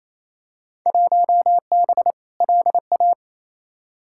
Morse code, heard as a keyed tone, about 28 wpm, 700 Hz. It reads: 1B LA
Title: Field Day Exchanges Encore - 28wpm